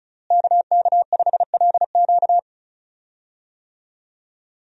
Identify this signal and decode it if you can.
Morse code, heard as a keyed tone, about 36 wpm, 700 Hz. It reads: KK5LQ
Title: USA Call Signs - 5 Characters - Repeated 2x - 35wpm